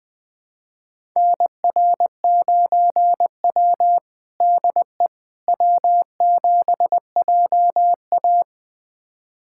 Morse code, heard as a keyed tone, about 20 wpm, 700 Hz. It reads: NR9W DE W7JA
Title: USA Callsign DE Callsign - 4 Characters - 20wpm